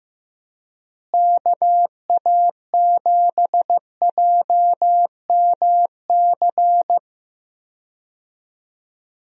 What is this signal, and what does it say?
Morse code, heard as a keyed tone, about 14 wpm, 700 Hz. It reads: KA7JMC
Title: USA Call Signs - 6 Characters - Repeated 2x - 15wpm